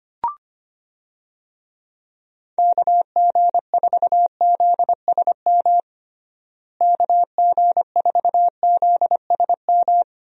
Morse code, heard as a keyed tone, about 25 wpm, 700 Hz. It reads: KG4ZSM KG4ZSM
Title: USA Call Signs - 6 Characters - Repeated 2x - 25wpm